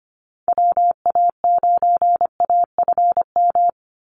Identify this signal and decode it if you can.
Morse code, heard as a keyed tone, about 25 wpm, 700 Hz. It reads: WA9AFM
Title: USA Call Signs 25wpm